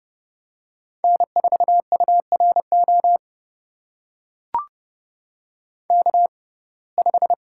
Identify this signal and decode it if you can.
Morse code, heard as a keyed tone, about 30 wpm, spaced as for 22 wpm, 700 Hz. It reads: N4URO K5
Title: USA Call Signs - 5 Characters - 6x Character Spacing - 30wpm